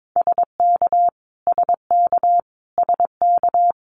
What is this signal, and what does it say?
Morse code, heard as a keyed tone, about 22 wpm, 700 Hz. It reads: SK SK SK
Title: Canadian Provinces and Territories - Repeated 3x - 22wpm